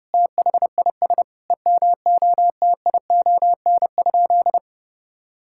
Morse code, heard as a keyed tone, about 30 wpm, 700 Hz. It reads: THIS EMOTION?
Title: Sentences from Top 1600 Words 30wpm